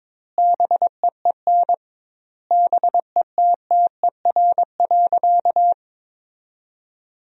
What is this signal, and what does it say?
Morse code, heard as a keyed tone, about 21 wpm, 700 Hz. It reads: BEEN BETTER.
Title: 2x Word-Spacing - Sentences from Top 500 Words - 22wpm